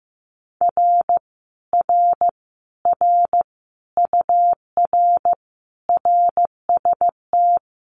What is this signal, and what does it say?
Morse code, heard as a keyed tone, about 15 wpm, 700 Hz. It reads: R R R UR RST
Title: Common QSO Phrases Encore 15wpm